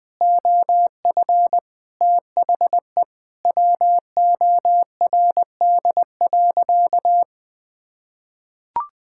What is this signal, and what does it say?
Morse code, heard as a keyed tone, about 21 wpm, 700 Hz. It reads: OF THE WORD.
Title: Sentences from Top 1000 Words - Review Part 6 - 20wpm